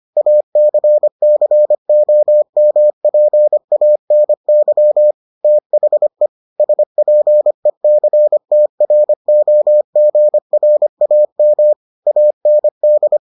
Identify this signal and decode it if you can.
Morse code, heard as a keyed tone, about 25 wpm, 600 Hz. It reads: ACCOMPANY THE SPECTROGRAM AND